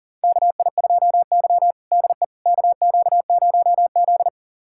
Morse code, heard as a keyed tone, about 40 wpm, 700 Hz. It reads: KI2Y DE KQ0Z
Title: USA Callsign DE Callsign - 4 Characters - 40wpm